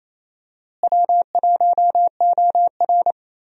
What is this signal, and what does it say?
Morse code, heard as a keyed tone, about 28 wpm, 700 Hz. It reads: W1OR
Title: USA Call Signs - 4 Characters - Repeated 3x - 28wpm